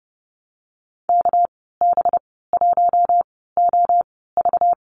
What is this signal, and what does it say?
Morse code, heard as a keyed tone, about 30 wpm, spaced as for 17 wpm, 700 Hz. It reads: KB1OV
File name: Lesson 37g - Letter x - Call Signs 30wpm